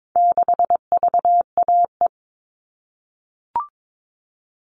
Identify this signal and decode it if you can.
Morse code, heard as a keyed tone, about 22 wpm, 700 Hz. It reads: 6VAE
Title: All Cut Numbers - 4-Digit 22wpm